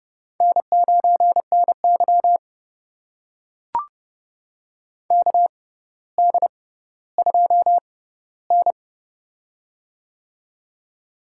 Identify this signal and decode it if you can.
Morse code, heard as a keyed tone, about 30 wpm, spaced as for 15 wpm, 700 Hz. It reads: N9NY KD2N
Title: USA Call Signs - 4 Characters - 6x Character Spacing - 30wpm